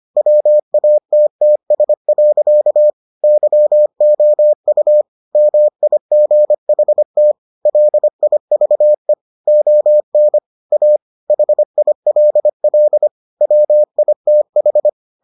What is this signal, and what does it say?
Morse code, heard as a keyed tone, about 25 wpm, 600 Hz. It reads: WATTS. YOU MIGHT LIVE ON A HILL WITH